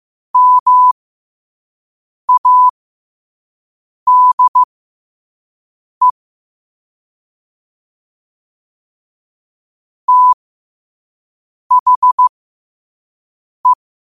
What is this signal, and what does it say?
Morse code, heard as a keyed tone, about 15 wpm, spaced as for 5 wpm, 1000 Hz. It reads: MADE THE
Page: New quotes every day in morse code at 5 Words per minute.